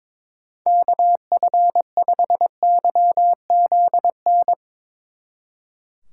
Morse code, heard as a keyed tone, about 29 wpm, 700 Hz. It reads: KF5YZN